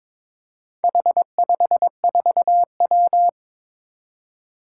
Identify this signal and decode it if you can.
Morse code, heard as a keyed tone, about 22 wpm, 700 Hz. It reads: H54W
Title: Four Challenging Letter-Number Combinations 22wpm